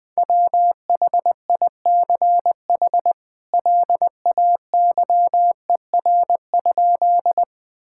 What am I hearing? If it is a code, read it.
Morse code, heard as a keyed tone, about 20 wpm, 700 Hz. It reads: WHICH LAYER?
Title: Sentences from Top 1600 Words 20wpm